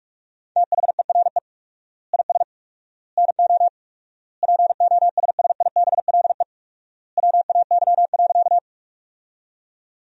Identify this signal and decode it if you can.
Morse code, heard as a keyed tone, about 46 wpm, 700 Hz. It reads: THERE IS NO POSSIBLE WAY.
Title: 4x Word-Spacing - Sentences from Top 500 Words - 45wpm